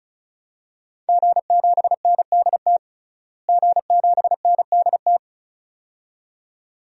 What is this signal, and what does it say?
Morse code, heard as a keyed tone, about 34 wpm, 700 Hz. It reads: G7NDT G7NDT
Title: DXCC-100 Call Signs - 5 Characters - Repeated 2x - 35wpm